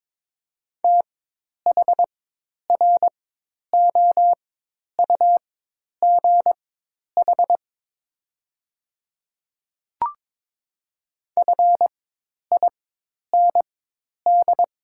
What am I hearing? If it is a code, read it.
Morse code, heard as a keyed tone, about 22 wpm, spaced as for 10 wpm, 700 Hz. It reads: THROUGH FIND
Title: 4x Character Spacing - Top 200 Words - 22wpm